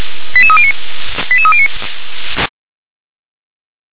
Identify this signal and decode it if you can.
Five-tone selective call (ZVEI), heard as quick runs of five tones, tones of about 70 ms. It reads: 88309 88309
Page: SDR/ZVEI